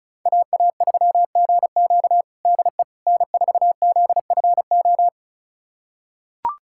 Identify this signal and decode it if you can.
Morse code, heard as a keyed tone, about 36 wpm, 700 Hz. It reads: AA3GQ DE N4ZFO
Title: USA Callsign DE Callsign - 5 Characters - 35wpm